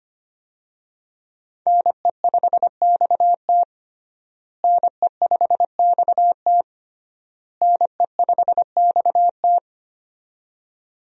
Morse code, heard as a keyed tone, about 25 wpm, 700 Hz. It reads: NE5XT NE5XT NE5XT
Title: USA Call Signs - 5 Characters - Repeated 3x - 25wpm